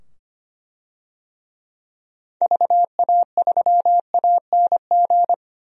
Morse code, heard as a keyed tone, about 25 wpm, 690 Hz. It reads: VA3ANG
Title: DXCC-100 Call Signs - 6 Characters - 25wpm